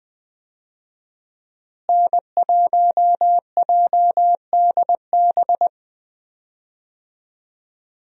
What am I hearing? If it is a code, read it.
Morse code, heard as a keyed tone, about 20 wpm, 700 Hz. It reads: N1JDB
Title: USA Call Signs - 5 Characters - Repeated 2x - 20wpm